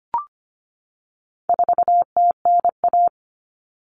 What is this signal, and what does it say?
Morse code, heard as a keyed tone, about 25 wpm, 700 Hz. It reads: 4TNA
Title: Common Cut Numbers - 4-Digit 25wpm